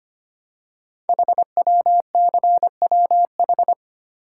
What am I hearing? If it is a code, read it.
Morse code, heard as a keyed tone, about 25 wpm, 700 Hz. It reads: HWCWH
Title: Code Groups - 5 Letters - 25wpm